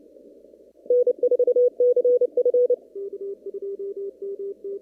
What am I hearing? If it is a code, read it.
Morse code, heard as a keyed tone, about 28 wpm, 490 Hz. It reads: N4CF
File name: CW / kt5j wpx cw-op-k5pi